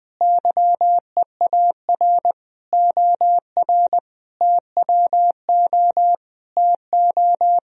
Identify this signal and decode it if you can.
Morse code, heard as a keyed tone, about 20 wpm, 700 Hz. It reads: YEAR OR TWO TO
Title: Sentences from Top 1000 Words - Review Part 1 - 20wpm